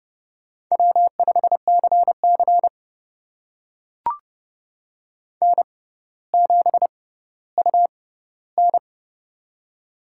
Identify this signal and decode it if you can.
Morse code, heard as a keyed tone, about 31 wpm, spaced as for 16 wpm, 700 Hz. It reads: W5CC N7UN
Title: USA Call Signs - 4 Characters - 6x Character Spacing - 30wpm